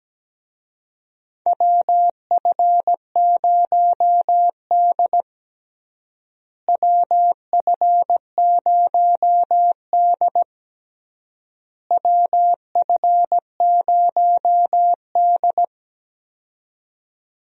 Morse code, heard as a keyed tone, about 17 wpm, 700 Hz. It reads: WF0D WF0D WF0D
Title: USA Call Signs - 4 Characters - Repeated 3x - 17wpm